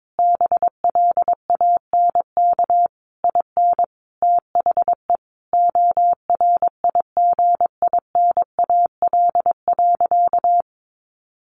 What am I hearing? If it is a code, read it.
Morse code, heard as a keyed tone, about 22 wpm, 700 Hz. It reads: BLANK IN THE ORIGINAL.
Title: Sentences from Top 1500 Words 22wpm